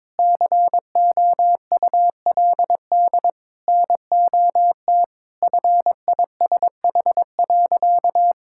Morse code, heard as a keyed tone, about 22 wpm, 700 Hz. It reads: COULD NOT FISH.
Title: Sentences from Top 1600 Words 22wpm